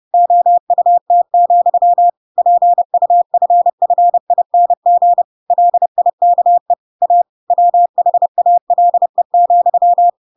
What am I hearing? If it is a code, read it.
Morse code, heard as a keyed tone, about 30 wpm, 700 Hz. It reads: OUT, PUFFING LIKE A WHALE,
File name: Pinocchio / Pinocchio ch6 30 30 ik1hge